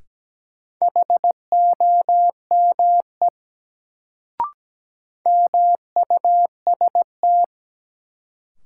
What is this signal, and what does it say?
Morse code, heard as a keyed tone, about 17 wpm, 700 Hz. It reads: HOME MUST
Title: Sentences from Top 200 Words - New Word Drill - 17wpm